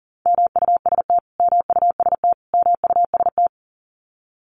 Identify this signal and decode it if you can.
Morse code, heard as a keyed tone, about 40 wpm, 700 Hz. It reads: MUST MUST MUST
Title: Top 500 Words Repeated 40wpm